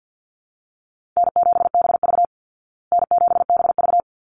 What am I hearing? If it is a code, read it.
Morse code, heard as a keyed tone, about 49 wpm, 700 Hz. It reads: N7BV N7BV
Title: Easy Call Signs Repeated 2x - 50wpm